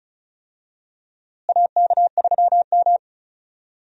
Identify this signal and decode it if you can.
Morse code, heard as a keyed tone, about 36 wpm, 700 Hz. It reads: AK3M
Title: USA Call Signs - 4 Characters - Repeated 2x - 35wpm